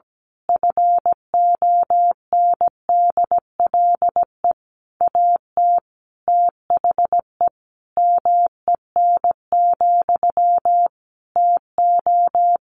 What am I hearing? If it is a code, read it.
Morse code, heard as a keyed tone, about 17 wpm, 700 Hz. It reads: FONDLE AT THE MEN, TO
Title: A Sound Of Thunder 17wpm